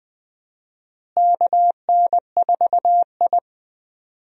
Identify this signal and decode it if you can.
Morse code, heard as a keyed tone, about 20 wpm, 700 Hz. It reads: KN4I
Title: USA Call Signs - 4 Characters - Repeated 3x - 20wpm